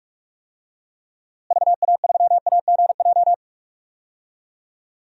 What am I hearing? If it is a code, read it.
Morse code, heard as a keyed tone, about 45 wpm, 700 Hz. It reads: VA3AGJ
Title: DXCC-100 Call Signs - 6 Characters - Repeated 3x - 45wpm